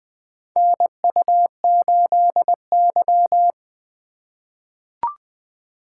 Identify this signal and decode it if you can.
Morse code, heard as a keyed tone, about 20 wpm, 700 Hz. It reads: NU8Y
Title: USA Call Signs - 4x Character Spacing - 20wpm